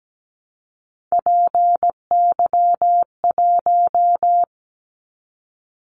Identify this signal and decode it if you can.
Morse code, heard as a keyed tone, about 17 wpm, 700 Hz. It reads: PY1
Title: Callsign Prefixes for Top 50 DXCC - 17wpm